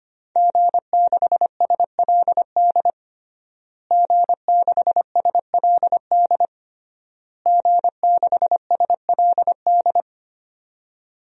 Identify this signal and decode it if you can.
Morse code, heard as a keyed tone, about 25 wpm, 700 Hz. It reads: G6SLD G6SLD G6SLD
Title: DXCC-100 Call Signs - 5 Characters - Repeated 3x - 25wpm